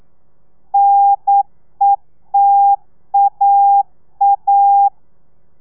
Morse code, heard as a keyed tone, about 9 wpm, 800 Hz. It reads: NETAA
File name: telegrafia / i2tfj / Lez2 40